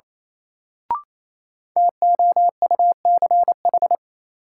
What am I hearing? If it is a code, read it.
Morse code, heard as a keyed tone, about 28 wpm, 700 Hz. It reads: TOUCH
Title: Sentences from Top 1200 Words - New Word Drill Alt - 28wpm